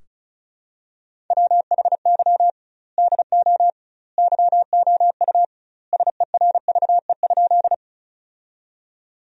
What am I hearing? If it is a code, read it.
Morse code, heard as a keyed tone, about 36 wpm, 700 Hz. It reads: WHY DO YOU SERVE?
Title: 2x Word-Spacing - Sentences from Top 400 Words - 35wpm